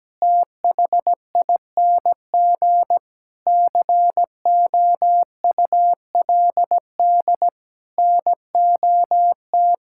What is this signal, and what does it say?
Morse code, heard as a keyed tone, about 17 wpm, 700 Hz. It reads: THING COULD NOT
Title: Sentences from Top 1000 Words - Review Part 6 - 17wpm